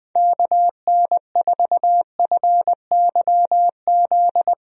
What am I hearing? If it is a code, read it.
Morse code, heard as a keyed tone, about 20 wpm, 700 Hz. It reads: KN4FYZ
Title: USA Call Signs - 6 Characters - 20wpm